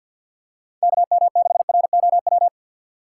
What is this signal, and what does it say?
Morse code, heard as a keyed tone, about 49 wpm, 700 Hz. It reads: KM6ROW
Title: USA Call Signs - 2x Character Spacing - 50wpm